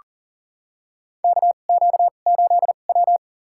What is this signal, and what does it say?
Morse code, heard as a keyed tone, about 40 wpm, spaced as for 29 wpm, 690 Hz. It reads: KQ8W
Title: USA Call Signs - 4 Characters - 2x Character Spacing - 40wpm